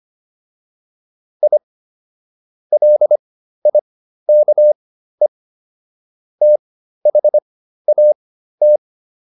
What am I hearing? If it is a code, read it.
Morse code, heard as a keyed tone, about 25 wpm, spaced as for 13 wpm, 600 Hz. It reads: I LIKE THAT